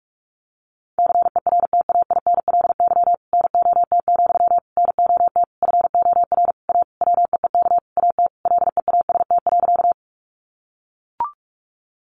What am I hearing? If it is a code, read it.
Morse code, heard as a keyed tone, about 44 wpm, 700 Hz. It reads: CERTAINLY NOT, NOT FOR A WEEK AT LEAST.
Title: Sentences from Top 900 Words 45wpm